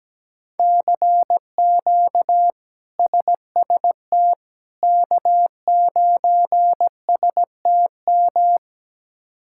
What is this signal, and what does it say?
Morse code, heard as a keyed tone, about 17 wpm, 700 Hz. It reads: CQ SST K9STM
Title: K1USN SST QSO Practice - 17wpm